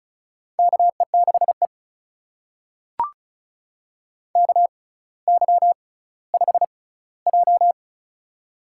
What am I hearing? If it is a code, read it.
Morse code, heard as a keyed tone, about 36 wpm, spaced as for 18 wpm, 700 Hz. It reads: KE6E KY5J
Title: USA Call Signs - 4 Characters - 6x Character Spacing - 35wpm